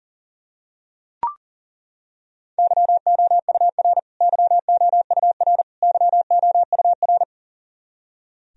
Morse code, heard as a keyed tone, about 40 wpm, 700 Hz. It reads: YOUR YOUR YOUR
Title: Top 100 Words Repeated 40wpm